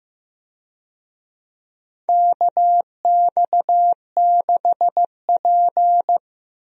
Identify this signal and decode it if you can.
Morse code, heard as a keyed tone, about 14 wpm, 700 Hz. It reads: KX6P
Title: USA Call Signs - 4 Characters - Repeated 3x - 15wpm